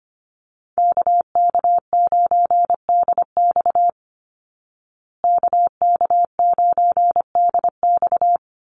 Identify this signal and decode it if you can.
Morse code, heard as a keyed tone, about 25 wpm, 700 Hz. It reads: KK9DX KK9DX
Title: Call Signs Repeated 2x - 25wpm